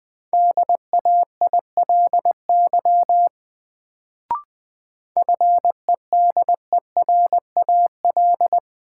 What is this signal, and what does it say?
Morse code, heard as a keyed tone, about 20 wpm, 700 Hz. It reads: DAILY FEDERAL
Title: Sentences from Top 1100 Words - New Word Drill Alt - 20wpm